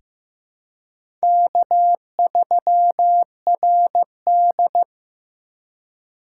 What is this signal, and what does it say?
Morse code, heard as a keyed tone, about 15 wpm, 700 Hz. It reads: K3RD
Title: USA Call Signs - 4 Characters - 3x Character Spacing - 15wpm